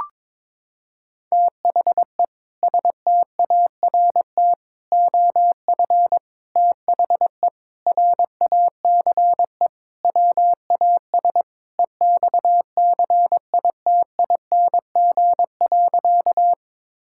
Morse code, heard as a keyed tone, about 22 wpm, 700 Hz. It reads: THE START OF THE RACE WAS EXCITING.
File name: practice / 22wpm / Sentences with 4-Word Phrases - 22wpm